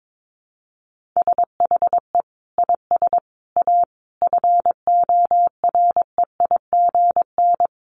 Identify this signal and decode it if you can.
Morse code, heard as a keyed tone, about 22 wpm, 700 Hz. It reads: SHE IS A FOREIGN
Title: Sentences from Top 1400 Words 22wpm